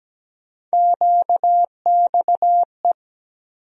Morse code, heard as a keyed tone, about 17 wpm, 700 Hz. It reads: QXE
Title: USA Call Sign Postfixes - 3 Letters 17wpm